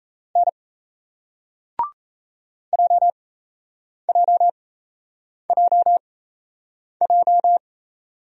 Morse code, heard as a keyed tone, about 38 wpm, spaced as for 8 wpm, 700 Hz. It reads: N JJJJ